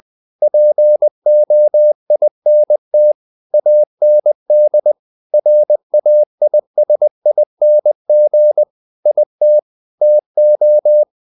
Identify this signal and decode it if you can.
Morse code, heard as a keyed tone, about 20 wpm, 600 Hz. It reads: POINT AND RAISING IT TO
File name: the old man and the sea part 2 20wpm